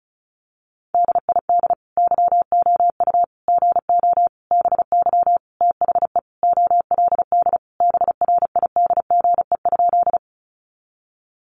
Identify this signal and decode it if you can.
Morse code, heard as a keyed tone, about 34 wpm, 700 Hz. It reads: DID YOU GO BY THE OLD BRIDGE?
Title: Sentences from Top 1200 Words 35wpm